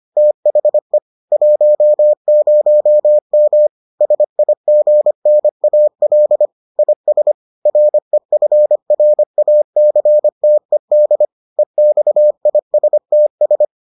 Morse code, heard as a keyed tone, about 25 wpm, 600 Hz. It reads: THE 10M SIGNAL IS REFRACTED EXISTS